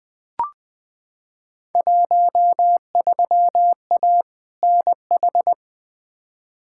Morse code, heard as a keyed tone, about 20 wpm, 700 Hz. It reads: 13A NH
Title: Field Day Exchanges Encore - 20wpm